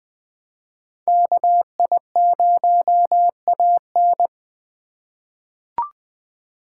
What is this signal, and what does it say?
Morse code, heard as a keyed tone, about 20 wpm, 700 Hz. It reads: KI0AN
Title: USA Call Signs - 5 Characters - 3x Character Spacing - 20wpm